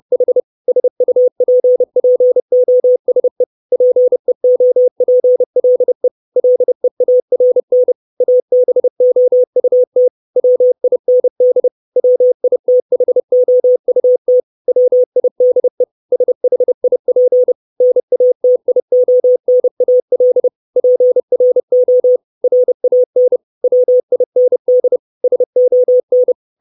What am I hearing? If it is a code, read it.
Morse code, heard as a keyed tone, about 30 wpm, 500 Hz. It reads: H SUPPOSE PEOPLE LEARN ABOUT WIND WITHOUT WIDE SHIP NATIONAL PRO RAN WIND SON
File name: 632 Words 30WPM